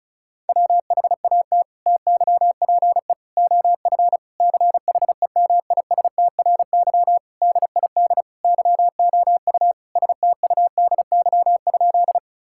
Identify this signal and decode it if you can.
Morse code, heard as a keyed tone, about 34 wpm, 700 Hz. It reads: WHAT TYPE OF CHEMISTRY DID YOU STUDY?
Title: Sentences from Top 1000 Words - Review Part 6 - 35wpm